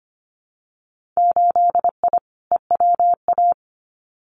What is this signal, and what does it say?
Morse code, heard as a keyed tone, about 25 wpm, 700 Hz. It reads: 8I EWA
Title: Winter Field Day Exchanges Encore 25wpm